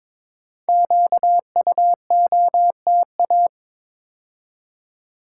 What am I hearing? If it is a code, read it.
Morse code, heard as a keyed tone, about 22 wpm, 700 Hz. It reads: QUOTA
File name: 5 Letter Words - First 3 Letters Given - 22wpm